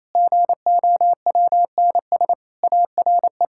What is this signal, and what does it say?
Morse code, heard as a keyed tone, about 28 wpm, 700 Hz. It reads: GOWNS ARE
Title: Pangram Challenge v2 - 28wpm